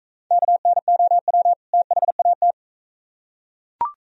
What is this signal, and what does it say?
Morse code, heard as a keyed tone, about 41 wpm, 700 Hz. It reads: KNOW THAT